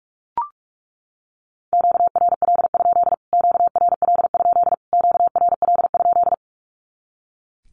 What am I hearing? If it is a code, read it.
Morse code, heard as a keyed tone, about 44 wpm, 700 Hz. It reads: QRL? QRL? QRL?
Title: Q-Codes Repeated 3x 45wpm